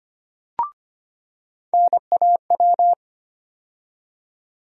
Morse code, heard as a keyed tone, about 25 wpm, 700 Hz. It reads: NAW
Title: USA Call Sign Postfixes - 3 Letters 25wpm